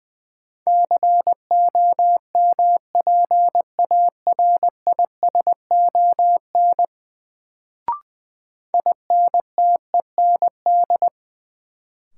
Morse code, heard as a keyed tone, about 20 wpm, 700 Hz. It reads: COMPARISON INTEND
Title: Sentences from Top 1600 Words - New Word Drill - 20wpm